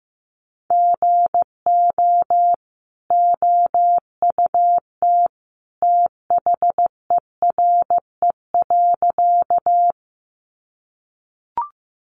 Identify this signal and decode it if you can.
Morse code, heard as a keyed tone, about 14 wpm, 700 Hz. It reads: GO OUT THERE.
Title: A Sound Of Thunder 15wpm